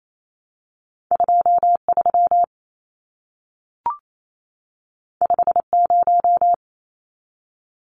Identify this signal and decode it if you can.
Morse code, heard as a keyed tone, about 28 wpm, 700 Hz. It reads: 23 50
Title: Numbers - 2-Digit 28wpm